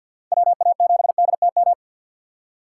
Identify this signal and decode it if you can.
Morse code, heard as a keyed tone, about 50 wpm, 700 Hz. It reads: WA7DTM
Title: USA Call Signs - 6 Characters - 3x Character Spacing - 50wpm